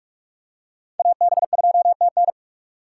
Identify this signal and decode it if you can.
Morse code, heard as a keyed tone, about 43 wpm, 700 Hz. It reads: AB2TN
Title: USA Call Signs - 5 Characters - 2x Character Spacing - 45wpm